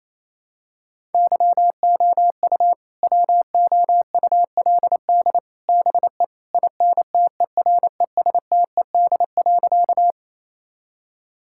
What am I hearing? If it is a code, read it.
Morse code, heard as a keyed tone, about 28 wpm, 700 Hz. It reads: YOU WOULD BE INTERESTED.
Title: Sentences from Top 1000 Words - Review Part 6 - 28wpm